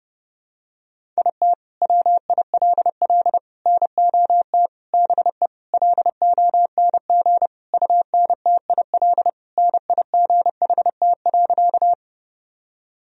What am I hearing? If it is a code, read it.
Morse code, heard as a keyed tone, about 30 wpm, 700 Hz. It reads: IT WILL NOT BE LONG UNTIL NIGHT.
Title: Sentences from Top 1000 Words - Review Part 1 - 30wpm